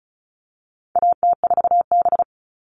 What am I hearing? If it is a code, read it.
Morse code, heard as a keyed tone, about 34 wpm, 700 Hz. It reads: AT4B
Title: All Cut Numbers - 4-Digit 35wpm